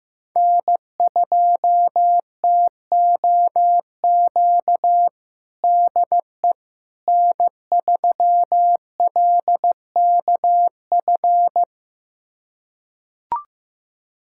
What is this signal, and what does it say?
Morse code, heard as a keyed tone, about 15 wpm, 700 Hz. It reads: N2TOQ DE N3LKF
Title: USA Callsign DE Callsign - 5 Characters - 15wpm